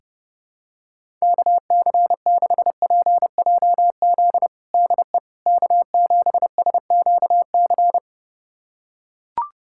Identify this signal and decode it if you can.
Morse code, heard as a keyed tone, about 32 wpm, 700 Hz. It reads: KC6PJZ DE K7SQC
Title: USA Callsign DE Callsign 30wpm